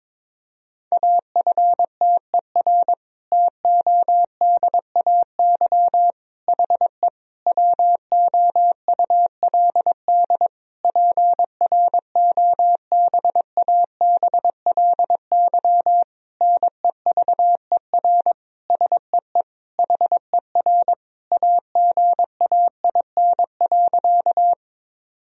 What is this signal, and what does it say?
Morse code, heard as a keyed tone, about 22 wpm, 700 Hz. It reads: AFTER TODAY HE WOULD PROBABLY NEVER SEE HER AGAIN.
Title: Sentences from Top 1000 Words - Review Part 4 - 22wpm